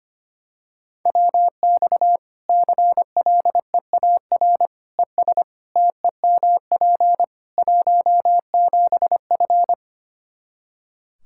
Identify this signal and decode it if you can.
Morse code, heard as a keyed tone, about 25 wpm, 700 Hz. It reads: WX CLEAR ES TEMP 17F
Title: Practice QSO - Exchange 2 - Info - 25wpm